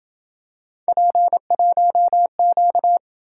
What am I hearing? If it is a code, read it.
Morse code, heard as a keyed tone, about 27 wpm, 700 Hz. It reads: P1Q